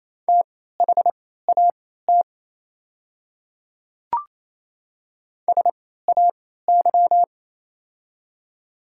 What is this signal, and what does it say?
Morse code, heard as a keyed tone, about 28 wpm, spaced as for 16 wpm, 700 Hz. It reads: THAT SAY
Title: 3x Character Spacing - Top 200 Words - 28wpm